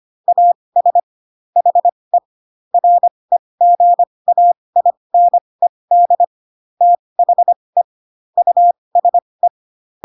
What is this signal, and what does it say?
Morse code, heard as a keyed tone, about 25 wpm, spaced as for 20 wpm, 700 Hz. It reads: AS HE REGAINED THE USE
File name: Pinocchio / Pinocchio ch1 15 25 ik1hge